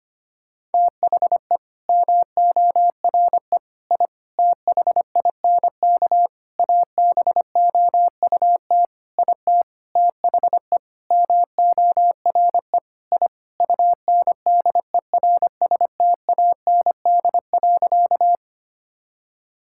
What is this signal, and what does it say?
Morse code, heard as a keyed tone, about 25 wpm, 700 Hz. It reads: THE MORE I THINK ABOUT IT THE MORE I UNDERSTAND.
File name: practice / 25wpm / Sentences with 9-Word Phrases - 25wpm